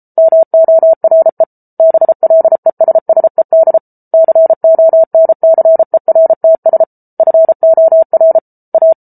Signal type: Morse code